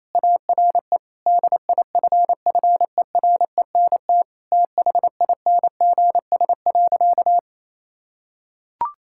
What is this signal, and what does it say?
Morse code, heard as a keyed tone, about 28 wpm, 700 Hz. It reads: ARE DIFFERENT THINGS.
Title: Sentences from Top 1000 Words - Review Part 6 - 28wpm